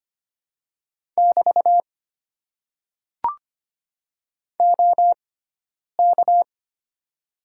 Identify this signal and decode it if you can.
Morse code, heard as a keyed tone, about 25 wpm, spaced as for 9 wpm, 690 Hz. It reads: <BT> OK
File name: Top 90 QSO Elements - 6x Character Spacing - 25wpm